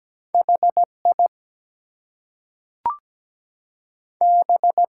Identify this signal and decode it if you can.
Morse code, heard as a keyed tone, about 17 wpm, 700 Hz. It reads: HI B
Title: Two Letter Words - 17wpm